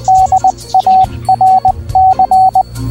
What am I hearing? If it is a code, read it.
Morse code, heard as a keyed tone, about 20 wpm, 760 Hz. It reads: DARC